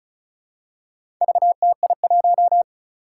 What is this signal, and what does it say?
Morse code, heard as a keyed tone, about 36 wpm, 700 Hz. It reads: VTI1
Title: Four Challenging Letter-Number Combinations 35wpm